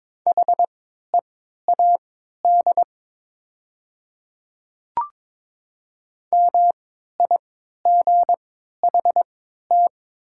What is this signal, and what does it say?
Morse code, heard as a keyed tone, about 22 wpm, spaced as for 12 wpm, 700 Hz. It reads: HEAD MIGHT
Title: 3x Character Spacing - Top 300 Words - 22wpm